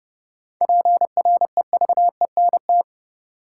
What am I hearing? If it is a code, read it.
Morse code, heard as a keyed tone, about 30 wpm, 700 Hz. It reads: PREVENT
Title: Sentences from Top 1100 Words - New Word Drill Alt - 30wpm